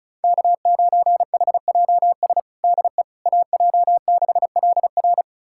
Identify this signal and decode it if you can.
Morse code, heard as a keyed tone, about 36 wpm, 700 Hz. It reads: K9HJS DE AJ6LR
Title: USA Callsign DE Callsign - 5 Characters - 35wpm